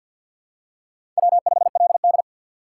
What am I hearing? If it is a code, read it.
Morse code, heard as a keyed tone, about 53 wpm, 700 Hz. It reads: W5LD
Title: USA Call Signs - 6x Character Spacing - 50wpm